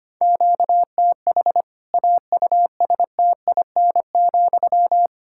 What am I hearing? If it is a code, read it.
Morse code, heard as a keyed tone, about 25 wpm, 700 Hz. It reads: QTH AUSTIN,
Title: Practice QSO - Exchange 1 - Beginning - 25wpm